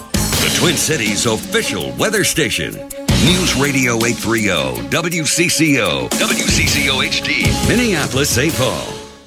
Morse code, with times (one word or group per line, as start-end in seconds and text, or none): none